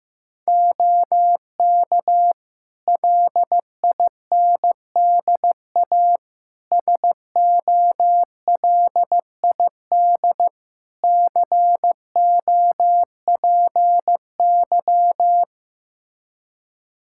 Morse code, heard as a keyed tone, about 15 wpm, 700 Hz. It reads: OK LINDA SOLID COPY
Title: Practice QSO - Exchange 3 - More Info - 15wpm